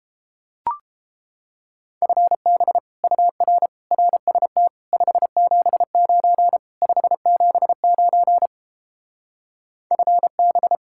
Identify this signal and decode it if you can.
Morse code, heard as a keyed tone, about 29 wpm, 700 Hz. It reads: FB UR RST 579 579 FB